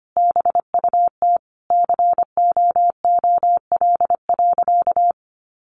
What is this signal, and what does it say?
Morse code, heard as a keyed tone, about 25 wpm, 700 Hz. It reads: BUT COOL.
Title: Sentences from Top 1200 Words 25wpm